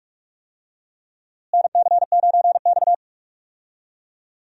Morse code, heard as a keyed tone, about 41 wpm, 700 Hz. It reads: NC9X
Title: USA Call Signs - 4 Characters - Repeated 3x - 45wpm